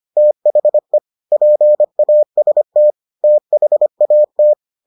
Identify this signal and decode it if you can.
Morse code, heard as a keyed tone, about 25 wpm, 600 Hz. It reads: THE PAST THAT